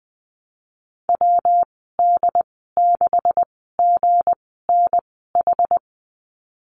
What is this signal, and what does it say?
Morse code, heard as a keyed tone, about 20 wpm, spaced as for 15 wpm, 700 Hz. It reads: WD6GNH
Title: 193 Lesson 31g - Letter v - Call Signs 20wpm